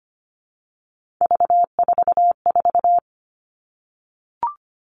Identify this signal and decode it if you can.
Morse code, heard as a keyed tone, about 25 wpm, 700 Hz. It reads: V44
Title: Three Challenging Letter-Number Combinations 25wpm